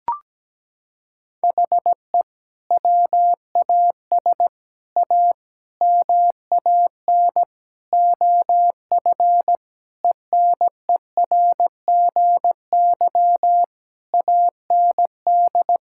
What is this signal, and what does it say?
Morse code, heard as a keyed tone, about 17 wpm, 700 Hz. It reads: HE WAS A MAN OF ENERGY AND
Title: Sentences from Top 1000 Words - Review Part 6 - 17wpm